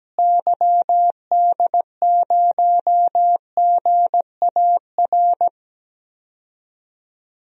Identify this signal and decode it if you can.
Morse code, heard as a keyed tone, about 17 wpm, 700 Hz. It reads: YD0GAR
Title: DXCC-100 Call Signs - 6 Characters - Repeated 2x - 17wpm